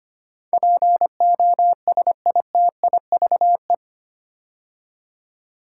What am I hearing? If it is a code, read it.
Morse code, heard as a keyed tone, about 25 wpm, 700 Hz. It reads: POSITIVE
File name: 8 Letter Words - First 3 Letters Given - 25wpm